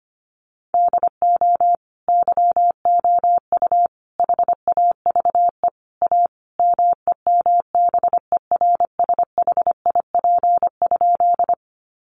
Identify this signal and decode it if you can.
Morse code, heard as a keyed tone, about 25 wpm, 700 Hz. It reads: DO YOU HAVE A MEMBERSHIP?
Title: Sentences from Top 1400 Words 25wpm